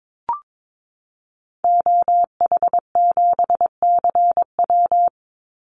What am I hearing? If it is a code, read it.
Morse code, heard as a keyed tone, about 22 wpm, 700 Hz. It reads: OH7CW
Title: Call Signs Repeated 3x - 22wpm